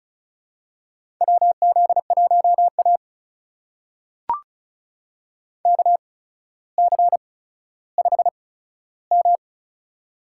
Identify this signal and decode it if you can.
Morse code, heard as a keyed tone, about 36 wpm, spaced as for 15 wpm, 700 Hz. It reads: WZ1A KC5M
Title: USA Call Signs - 4 Characters - 8x Character Spacing - 35wpm